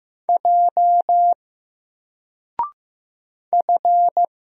Morse code, heard as a keyed tone, about 15 wpm, 700 Hz. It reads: J F
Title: Single Character (Letter:Number:Punctuation:Prosign) Alt 15wpm